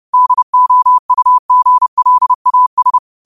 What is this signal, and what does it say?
Morse code, heard as a keyed tone, about 30 wpm, 1000 Hz. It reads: DOUGLAS